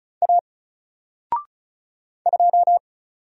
Morse code, heard as a keyed tone, about 35 wpm, 700 Hz. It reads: A 2
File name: Single-Letter-Number-35wpm